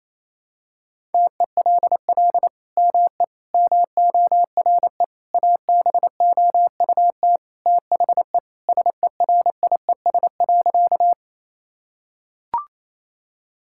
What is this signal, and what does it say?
Morse code, heard as a keyed tone, about 27 wpm, 700 Hz. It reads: TELL ME MORE ABOUT THE SERIES.
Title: Sentences from Top 1000 Words - Review Part 5 - 28wpm